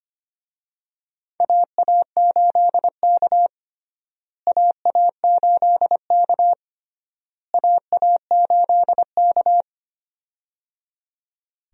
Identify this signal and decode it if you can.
Morse code, heard as a keyed tone, about 25 wpm, 700 Hz. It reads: AA8K AA8K AA8K
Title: USA Call Signs - 4 Characters - Repeated 3x - 25wpm